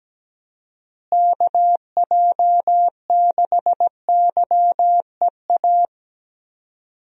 Morse code, heard as a keyed tone, about 17 wpm, 700 Hz. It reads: KJ6YEA
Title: USA Call Signs - 6 Characters - Repeated 2x - 17wpm